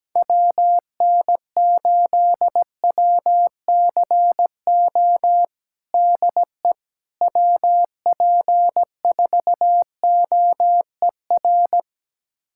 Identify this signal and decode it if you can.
Morse code, heard as a keyed tone, about 17 wpm, 700 Hz. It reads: WN8WCO DE WP4OER
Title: USA Callsign DE Callsign - 6 Characters - 17wpm